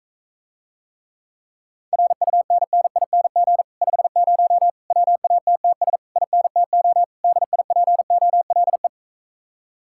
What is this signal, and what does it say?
Morse code, heard as a keyed tone, about 45 wpm, 700 Hz. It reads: RUNNING 50 WATTS INTO DIPOLE